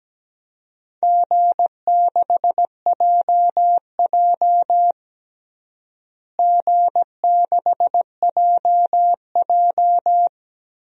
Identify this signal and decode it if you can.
Morse code, heard as a keyed tone, about 17 wpm, 700 Hz. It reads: G6JJ G6JJ
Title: DXCC-100 Call Signs - 4 Characters - Repeated 2x - 17wpm